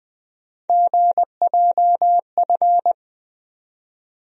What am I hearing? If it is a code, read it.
Morse code, heard as a keyed tone, about 20 wpm, 700 Hz. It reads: GJF
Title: USA Call Sign Postfixes - 3 Letters 20wpm